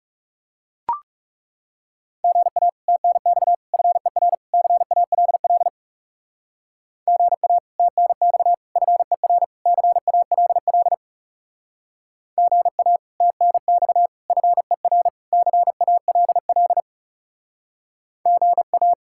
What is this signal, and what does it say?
Morse code, heard as a keyed tone, about 37 wpm, 700 Hz. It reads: GA TNX FER CALL GA TNX FER CALL GA TNX FER CALL GA
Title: Speed-Racing Common QSO Phrases - 30wpm